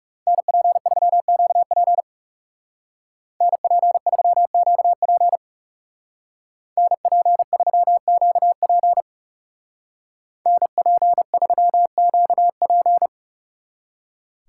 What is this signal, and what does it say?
Morse code, heard as a keyed tone, about 37 wpm, 700 Hz. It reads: NP3QP NP3QP NP3QP NP3QP
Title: Speed Racing - USA 2x2 Call Signs 30wpm